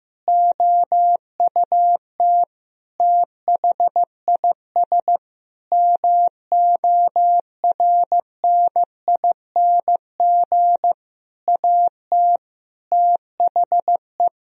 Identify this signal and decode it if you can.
Morse code, heard as a keyed tone, about 15 wpm, 700 Hz. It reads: OUT THIS MORNING AT THE
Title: Sentences from Top 1000 Words - Review Part 2 - 15wpm